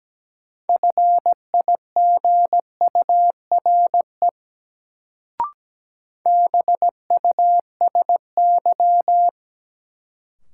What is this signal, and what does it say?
Morse code, heard as a keyed tone, about 17 wpm, 700 Hz. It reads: FIGURE BUSY
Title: Sentences from Top 1000 Words - Review Part 2 - New Word Drill - 17wpm